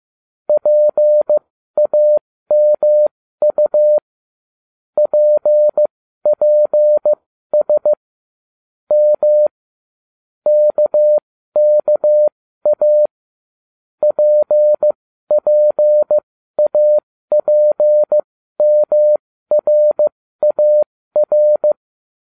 Morse code, heard as a keyed tone, about 15 wpm, spaced as for 12 wpm, 600 Hz. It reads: PAMU PPS M KKA PPAPMRAR